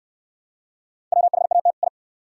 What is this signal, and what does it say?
Morse code, heard as a keyed tone, about 63 wpm, 690 Hz. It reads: WHAT I